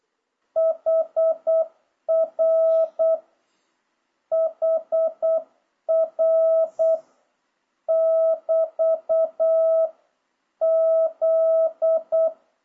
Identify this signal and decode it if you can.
Morse code, heard as a keyed tone, about 8 wpm, 650 Hz. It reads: HR HR <BT> Z
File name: jh8cbh / 2026cw